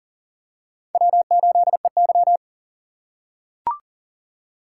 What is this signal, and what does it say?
Morse code, heard as a keyed tone, about 37 wpm, 700 Hz. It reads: W8EY
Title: USA Call Signs - 4x Character Spacing - 40wpm